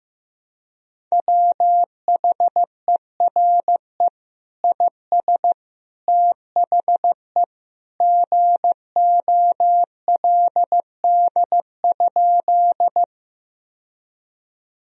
Morse code, heard as a keyed tone, about 15 wpm, 700 Hz. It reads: WHERE IS THE GOLD?
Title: Sentences from Top 1000 Words - Review Part 3 - 15wpm